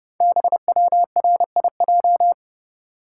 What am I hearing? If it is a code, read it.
Morse code, heard as a keyed tone, about 30 wpm, 700 Hz. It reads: BWRIJ
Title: Code Groups - 5 Letters - 30wpm